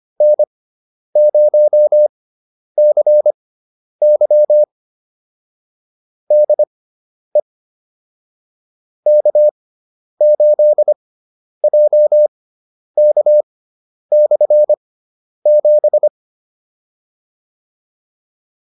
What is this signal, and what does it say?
Morse code, heard as a keyed tone, about 25 wpm, spaced as for 10 wpm, 600 Hz. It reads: N0CY DE K8JK/7